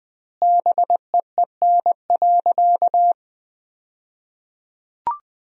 Morse code, heard as a keyed tone, about 20 wpm, 700 Hz. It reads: BEEN.
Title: 2x Word-Spacing - Sentences from Top 300 Words - 20wpm